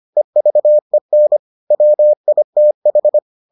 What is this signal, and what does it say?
Morse code, heard as a keyed tone, about 25 wpm, 600 Hz. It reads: EVEN WITH